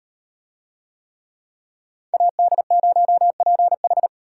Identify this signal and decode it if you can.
Morse code, heard as a keyed tone, about 38 wpm, 700 Hz. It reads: AD0PH